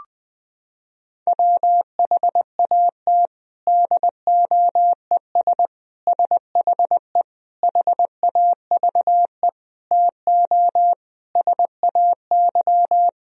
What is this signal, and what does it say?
Morse code, heard as a keyed tone, about 20 wpm, 700 Hz. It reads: WHAT DOES SHE HAVE TO SAY
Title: Sentences from Top 1000 Words - Review Part 3 - 20wpm